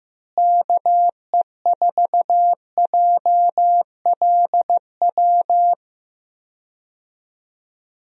Morse code, heard as a keyed tone, about 15 wpm, 700 Hz. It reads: KE4JLW
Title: USA Call Signs - 6 Characters - Repeated 2x - 15wpm